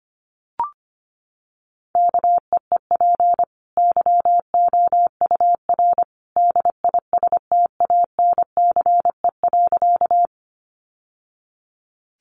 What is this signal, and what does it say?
Morse code, heard as a keyed tone, about 25 wpm, 700 Hz. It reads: KEEP YOUR DISTANCE.
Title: Sentences from Top 1200 Words 25wpm